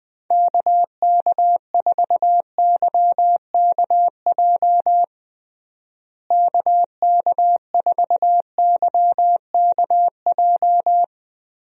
Morse code, heard as a keyed tone, about 20 wpm, 700 Hz. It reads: KK4YKJ KK4YKJ
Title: USA Call Signs - 6 Characters - Repeated 2x - 20wpm